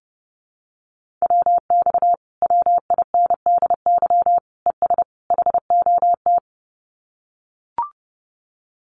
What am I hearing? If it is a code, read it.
Morse code, heard as a keyed tone, about 30 wpm, 700 Hz. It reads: WX WINDY ES HOT
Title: Common QSO Phrases Encore 30wpm